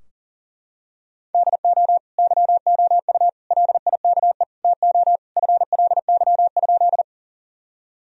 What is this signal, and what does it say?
Morse code, heard as a keyed tone, about 40 wpm, 700 Hz. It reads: DO YOU LIKE TO FLY?
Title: Sentences from Top 1000 Words - Review Part 2 - 40wpm